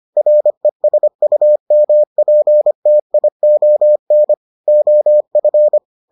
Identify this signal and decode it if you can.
Morse code, heard as a keyed tone, about 25 wpm, 600 Hz. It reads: RESUMPTION OF